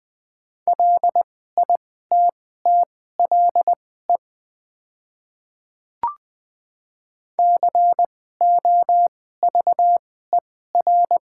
Morse code, heard as a keyed tone, about 20 wpm, spaced as for 14 wpm, 690 Hz. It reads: LITTLE COVER
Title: 2x Character Spacing - Top 300 Words - 20wpm